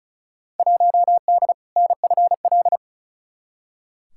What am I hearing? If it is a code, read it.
Morse code, heard as a keyed tone, about 34 wpm, 700 Hz. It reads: 1D NFL
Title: Field Day Exchanges - 35wpm